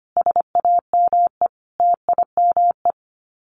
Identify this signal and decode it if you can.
Morse code, heard as a keyed tone, about 25 wpm, 700 Hz. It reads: SAME TIME
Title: Sets of 5 Words Encore 25wpm